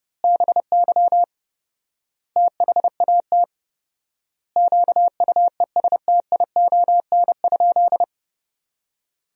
Morse code, heard as a keyed tone, about 30 wpm, 700 Hz. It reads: BY THAT QUESTION?
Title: 4x Word-Spacing - Sentences from Top 400 Words - 30wpm